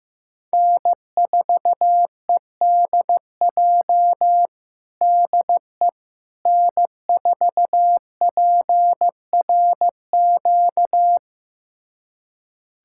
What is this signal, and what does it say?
Morse code, heard as a keyed tone, about 15 wpm, 700 Hz. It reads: N4EDJ DE N4PRQ
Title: USA Callsign DE Callsign - 5 Characters - 15wpm